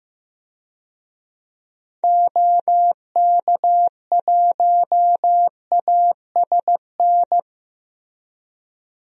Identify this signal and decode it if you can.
Morse code, heard as a keyed tone, about 15 wpm, 700 Hz. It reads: OK1ASN
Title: DXCC-100 Call Signs - 6 Characters - Repeated 3x - 15wpm